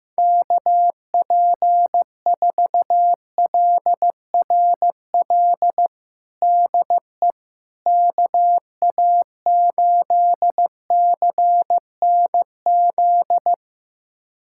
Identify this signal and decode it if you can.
Morse code, heard as a keyed tone, about 15 wpm, 700 Hz. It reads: KP4LRL DE KA8CNZ
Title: USA Callsign DE Callsign 15wpm